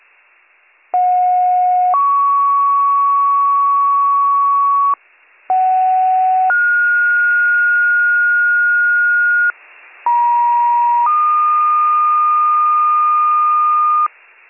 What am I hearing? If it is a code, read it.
Two-tone page, heard as a one-second tone, then a three-second tone, about 730 then 1120 Hz, about 750 then 1430 Hz, about 950 then 1220 Hz. It is heard from 0.9 to 14.1 s.